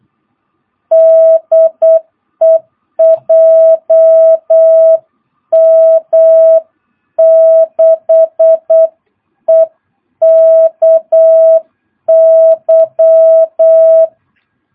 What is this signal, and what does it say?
Morse code, heard as a keyed tone, about 8 wpm, 650 Hz. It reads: DE JM6EKY